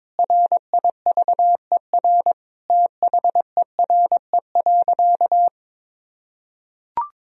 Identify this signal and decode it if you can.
Morse code, heard as a keyed tone, about 22 wpm, 700 Hz. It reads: RIVER THERE.
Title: Sentences from Top 1000 Words - Review Part 3 - 22wpm